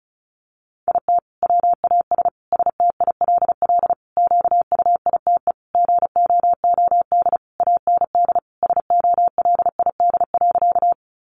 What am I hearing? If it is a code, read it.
Morse code, heard as a keyed tone, about 34 wpm, 700 Hz. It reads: IT WAS STILL QUITE GOOD AND SOLID.
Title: Sentences from Top 1500 Words 35wpm